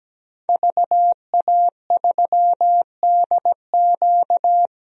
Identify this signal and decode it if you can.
Morse code, heard as a keyed tone, about 17 wpm, 700 Hz. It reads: VA3DQ
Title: DXCC-100 Call Signs - 5 Characters - Repeated 3x - 17wpm